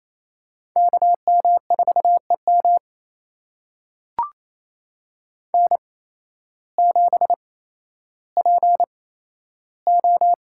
Morse code, heard as a keyed tone, about 28 wpm, spaced as for 13 wpm, 700 Hz. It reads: KM4EM N7PO
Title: USA Call Signs - 5 Characters - 8x Character Spacing - 28wpm